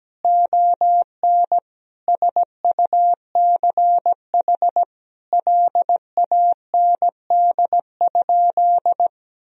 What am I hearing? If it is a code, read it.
Morse code, heard as a keyed tone, about 17 wpm, 700 Hz. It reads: ON SUCH LAND?
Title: Sentences from Top 1000 Words - Review Part 1 - 17wpm